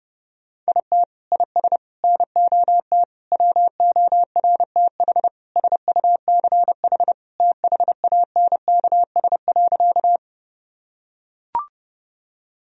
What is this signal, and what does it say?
Morse code, heard as a keyed tone, about 30 wpm, 700 Hz. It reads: IT IS NOT WORTH SUCH THANKS.
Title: Sentences from Top 1000 Words - Review Part 3 - 30wpm